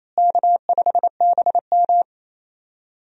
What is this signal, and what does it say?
Morse code, heard as a keyed tone, about 27 wpm, 700 Hz. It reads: K5BM
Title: USA Call Signs - 4 Characters - Repeated 3x - 28wpm